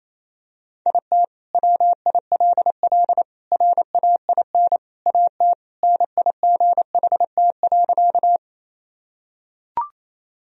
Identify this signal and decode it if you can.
Morse code, heard as a keyed tone, about 28 wpm, 700 Hz. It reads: IT WILL RAIN AT NIGHT.
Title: Sentences from Top 1000 Words - Review Part 1 - 28wpm